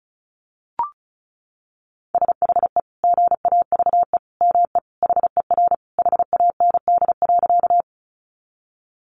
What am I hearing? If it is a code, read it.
Morse code, heard as a keyed tone, about 34 wpm, 700 Hz. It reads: SHE GAVE ME HER HAND.
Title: Sentences from Top 500 Words 35wpm